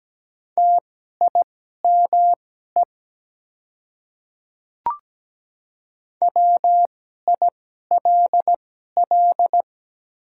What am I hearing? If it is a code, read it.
Morse code, heard as a keyed tone, about 17 wpm, spaced as for 12 wpm, 700 Hz. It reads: TIME WILL
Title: 2x Character Spacing - Top 100 Words - 17wpm